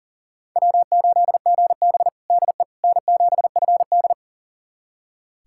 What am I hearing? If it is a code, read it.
Morse code, heard as a keyed tone, about 40 wpm, 700 Hz. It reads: W8GB DE N7FD
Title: USA Callsign DE Callsign - 4 Characters - 40wpm